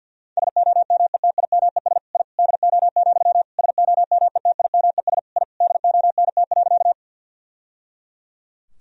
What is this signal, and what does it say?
Morse code, heard as a keyed tone, about 50 wpm, 700 Hz. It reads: SOMETIMES I DO, SOMETIMES I DONT.
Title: Sentences from Top 1000 Words - Review Part 3 - 50wpm